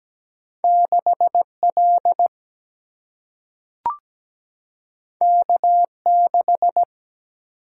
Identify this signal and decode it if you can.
Morse code, heard as a keyed tone, about 17 wpm, 700 Hz. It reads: 6L K6
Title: Single-Letter Single-Number 17wpm